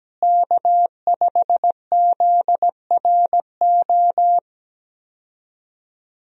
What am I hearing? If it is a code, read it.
Morse code, heard as a keyed tone, about 17 wpm, 700 Hz. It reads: K5ZRO
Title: USA Call Signs - 5 Characters - Repeated 3x - 17wpm